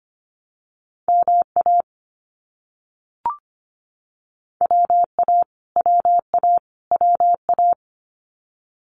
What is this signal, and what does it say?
Morse code, heard as a keyed tone, about 25 wpm, 700 Hz. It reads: MA WA WA WA
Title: US States Repeated 3x 25wpm